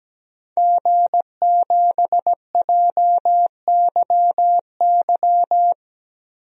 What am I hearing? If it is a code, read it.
Morse code, heard as a keyed tone, about 17 wpm, 700 Hz. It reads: G7JYY
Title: DXCC-100 Call Signs - 5 Characters - Repeated 2x - 17wpm